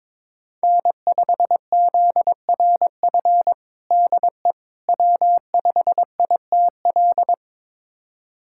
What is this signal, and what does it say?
Morse code, heard as a keyed tone, about 22 wpm, 700 Hz. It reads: N5ZRF DE W5ITL
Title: USA Callsign DE Callsign - 5 Characters - 22wpm